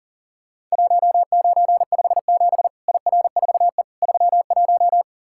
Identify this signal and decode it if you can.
Morse code, heard as a keyed tone, about 40 wpm, 700 Hz. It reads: 1957 IR4E 31
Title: Worked All Europe DX Contest - QTC Exchange - 40wpm